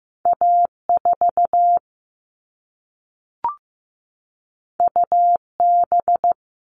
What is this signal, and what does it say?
Morse code, heard as a keyed tone, about 15 wpm, 700 Hz. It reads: A4 UB
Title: All Cut Numbers - 2-Digit 15wpm